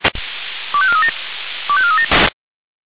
Five-tone selective call (ZVEI), heard as quick runs of five tones, tones of about 70 ms. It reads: 25437 25437